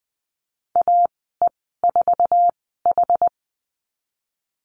Morse code, heard as a keyed tone, about 20 wpm, spaced as for 15 wpm, 700 Hz. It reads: AE4H
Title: 088 Lesson 15g - Number 5 - Call Signs 20wpm